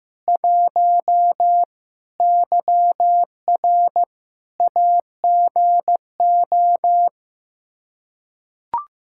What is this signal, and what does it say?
Morse code, heard as a keyed tone, about 15 wpm, 700 Hz. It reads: 1 YR AGO
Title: Practice QSO - Exchange 3 - More Info - 15wpm